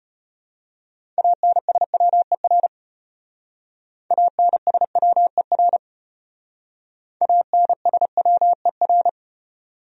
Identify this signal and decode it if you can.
Morse code, heard as a keyed tone, about 30 wpm, 700 Hz. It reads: ANSWER ANSWER ANSWER
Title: Speed-Racing Top 300 Words 25wpm